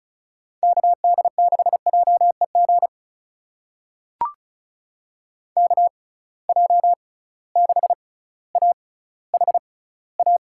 Morse code, heard as a keyed tone, about 35 wpm, spaced as for 18 wpm, 700 Hz. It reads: KD6JEG KJ6AHA
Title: USA Call Signs - 6 Characters - 6x Character Spacing - 35wpm